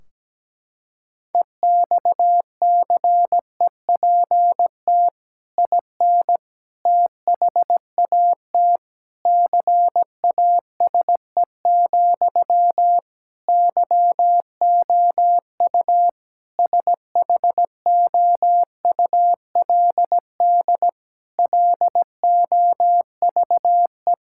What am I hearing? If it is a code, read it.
Morse code, heard as a keyed tone, about 17 wpm, 700 Hz. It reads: EXCEPT IN THAT CASE, YOU SHOULD LOVE
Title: Sentences from Top 1600 Words 17wpm